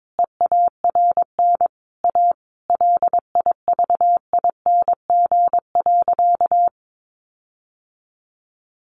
Morse code, heard as a keyed tone, about 22 wpm, 700 Hz. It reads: EARN A LIVING.
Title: Sentences from Top 1500 Words 22wpm